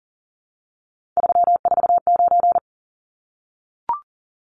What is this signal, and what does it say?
Morse code, heard as a keyed tone, about 40 wpm, 700 Hz. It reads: 349
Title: Numbers - 3-Digit 40wpm